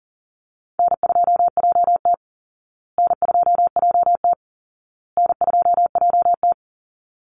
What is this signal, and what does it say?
Morse code, heard as a keyed tone, about 40 wpm, 700 Hz. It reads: N2JT N2JT N2JT
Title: Easy Call Signs Repeated 3x - 40wpm